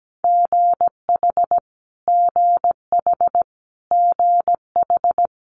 Morse code, heard as a keyed tone, about 17 wpm, 700 Hz. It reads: GH GH GH
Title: Multi-letter Phonemes Repeated 3x 17wpm